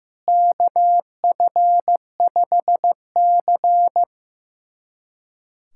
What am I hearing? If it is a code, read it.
Morse code, heard as a keyed tone, about 15 wpm, 700 Hz. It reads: KF5C
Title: USA Callsign DE Callsign - 4 Characters - 15wpm